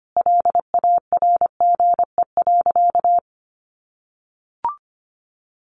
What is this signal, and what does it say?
Morse code, heard as a keyed tone, about 25 wpm, 700 Hz. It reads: LARGE.
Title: Sentences from Top 700 Words 25wpm